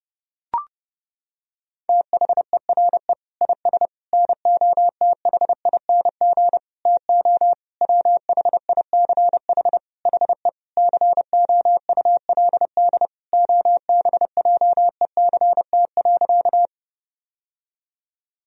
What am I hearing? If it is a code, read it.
Morse code, heard as a keyed tone, about 30 wpm, 700 Hz. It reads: THERE IS NOTHING TO WHICH HE COULD OBJECT.
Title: Sentences from Top 1000 Words - Review Part 3 - 30wpm